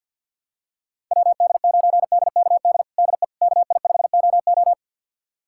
Morse code, heard as a keyed tone, about 47 wpm, 700 Hz. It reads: WD9DKD DE KI5OO
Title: USA Callsign DE Callsign 50wpm